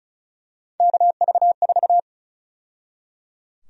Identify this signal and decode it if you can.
Morse code, heard as a keyed tone, about 36 wpm, 700 Hz. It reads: KV4
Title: USA Call Sign Prefixes - 35wpm